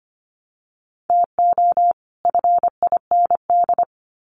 Morse code, heard as a keyed tone, about 25 wpm, 700 Hz. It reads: TO FIND
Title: Sets of 2 Words 25wpm